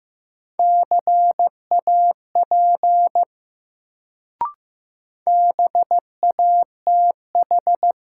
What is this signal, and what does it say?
Morse code, heard as a keyed tone, about 15 wpm, 700 Hz. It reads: CAP BATH
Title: Sentences from Top 1600 Words - New Word Drill Alt - 15wpm